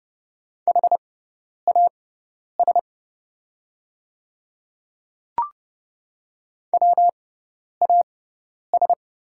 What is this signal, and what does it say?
Morse code, heard as a keyed tone, about 30 wpm, spaced as for 10 wpm, 700 Hz. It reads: HAS WAS
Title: 6x Character Spacing - Top 100 Words - 30wpm